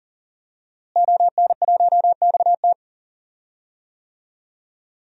Morse code, heard as a keyed tone, about 40 wpm, 700 Hz. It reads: ON1XT
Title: DXCC-100 Call Signs - 5 Characters - Repeated 2x - 40wpm